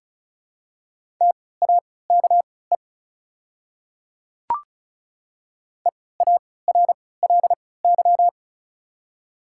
Morse code, heard as a keyed tone, about 34 wpm, spaced as for 20 wpm, 690 Hz. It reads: TAKE EARLY
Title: 3x Character Spacing - Top 500 Words - 35wpm